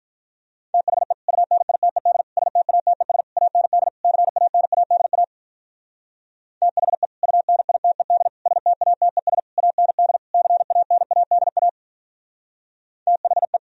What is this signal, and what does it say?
Morse code, heard as a keyed tone, about 43 wpm, 700 Hz. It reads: THE UNITED STATES AND CANADA THE UNITED STATES AND CANADA THE
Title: Speed-Racing - Sets of 5 Words 35wpm